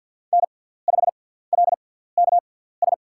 Morse code, heard as a keyed tone, about 53 wpm, spaced as for 17 wpm, 700 Hz. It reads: N5LKS
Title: USA Call Signs - 6x Character Spacing - 50wpm